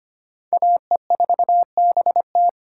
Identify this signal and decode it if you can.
Morse code, heard as a keyed tone, about 25 wpm, 700 Hz. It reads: AE4BT
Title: USA Call Signs - 5 Characters - 8x Character Spacing - 25wpm